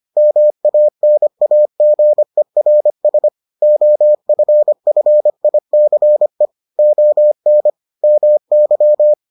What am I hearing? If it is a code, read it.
Morse code, heard as a keyed tone, about 25 wpm, 600 Hz. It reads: MANAGERS OFFICE ON MY